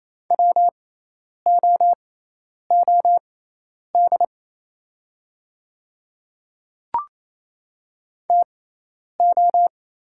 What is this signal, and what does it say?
Morse code, heard as a keyed tone, about 28 wpm, spaced as for 10 wpm, 700 Hz. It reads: WOOD TO
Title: 6x Character Spacing - Top 300 Words - 28wpm